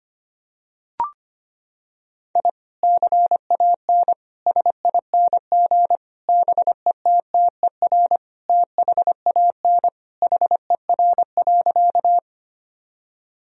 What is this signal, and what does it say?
Morse code, heard as a keyed tone, about 25 wpm, 700 Hz. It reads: I CAN SING BETTER THAN HER.
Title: Sentences from Top 1000 Words - Review Part 2 - 25wpm